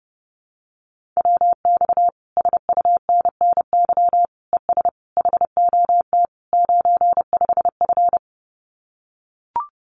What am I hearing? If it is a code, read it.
Morse code, heard as a keyed tone, about 28 wpm, 700 Hz. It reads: WX SUNNY ES HOT 95F
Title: Common QSO Phrases Encore 30wpm